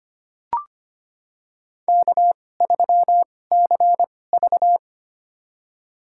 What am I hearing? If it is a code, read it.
Morse code, heard as a keyed tone, about 25 wpm, spaced as for 18 wpm, 700 Hz. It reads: K3CV
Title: USA Call Signs - 4 Characters - 2x Character Spacing - 25wpm